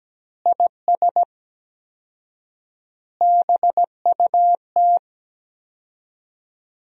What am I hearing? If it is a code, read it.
Morse code, heard as a keyed tone, about 17 wpm, 700 Hz. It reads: IS BUT
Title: 4x Word-Spacing - Sentences from Top 100 Words - 17wpm